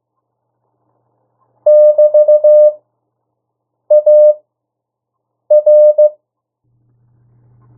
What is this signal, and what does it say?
Morse code, heard as a keyed tone, about 15 wpm, spaced as for 6 wpm, 1800 Hz. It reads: <BT>AR
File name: urok 1